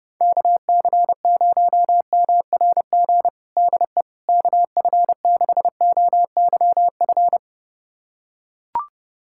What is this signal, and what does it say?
Morse code, heard as a keyed tone, about 32 wpm, 700 Hz. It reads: KC0MRG DE KF6OYF
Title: USA Callsign DE Callsign - 6 Characters - 30wpm